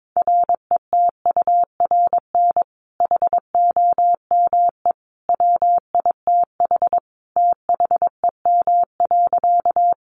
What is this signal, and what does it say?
Morse code, heard as a keyed tone, about 22 wpm, 700 Hz. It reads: RETURN HOME WITH THEM.
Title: Sentences from Top 800 Words 22wpm